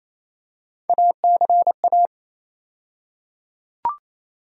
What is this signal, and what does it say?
Morse code, heard as a keyed tone, about 28 wpm, 700 Hz. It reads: ACA
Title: USA Call Sign Postfixes - 3 Letters 28wpm